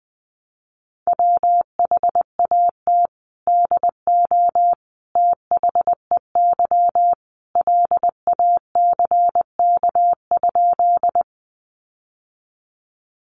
Morse code, heard as a keyed tone, about 20 wpm, 700 Hz. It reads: WHAT DO THEY LACK?
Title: Sentences from Top 1100 Words 20wpm